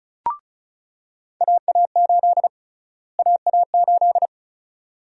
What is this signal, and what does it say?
Morse code, heard as a keyed tone, about 36 wpm, 700 Hz. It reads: AA8 AA8
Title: USA Call Sign Prefixes - Repeated 2x - 35wpm